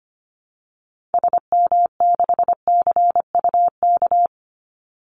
Morse code, heard as a keyed tone, about 25 wpm, 700 Hz. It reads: SM6CUK
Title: Call Signs Repeated 2x - 25wpm